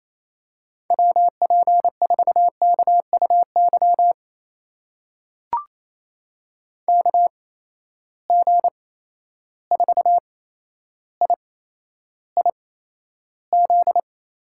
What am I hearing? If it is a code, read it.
Morse code, heard as a keyed tone, about 28 wpm, spaced as for 12 wpm, 700 Hz. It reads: WP4KUY KG4IIZ
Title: USA Call Signs - 6 Characters - 8x Character Spacing - 28wpm